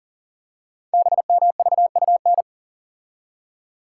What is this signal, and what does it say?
Morse code, heard as a keyed tone, about 40 wpm, 700 Hz. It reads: BMVUN
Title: Code Groups - 5 Letters - 40wpm